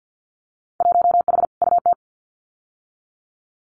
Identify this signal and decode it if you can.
Morse code, heard as a keyed tone, about 50 wpm, 700 Hz. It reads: 1H UT
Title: Winter Field Day Exchanges 50wpm